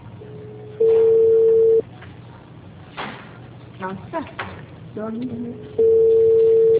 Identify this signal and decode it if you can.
Ringback tone (China, 450 Hz), with speech.